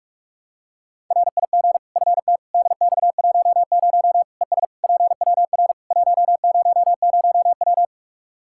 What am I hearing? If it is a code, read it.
Morse code, heard as a keyed tone, about 43 wpm, 700 Hz. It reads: RIG FT DX10 ES PWR 100W
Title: Practice QSO - Exchange 2 - Info - 45wpm